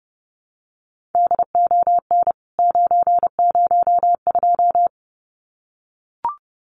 Morse code, heard as a keyed tone, about 28 wpm, 700 Hz. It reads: DON 902
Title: CWT Exchange 30wpm